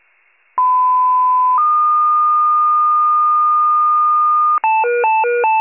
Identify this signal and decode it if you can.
Two-tone page, heard as a one-second tone, then a three-second tone, about 1010 then 1220 Hz. It is heard from 0.6 to 4.6 s.
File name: wcalert